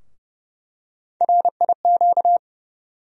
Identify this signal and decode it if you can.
Morse code, heard as a keyed tone, about 30 wpm, 700 Hz. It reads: RIQ
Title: USA Call Sign Postfixes - 3 Letters 30wpm